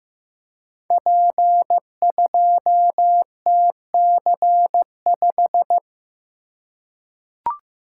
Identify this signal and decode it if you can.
Morse code, heard as a keyed tone, about 15 wpm, 700 Hz. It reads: P2TC5
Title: Code Groups - 5 Letter:Numbers - 15wpm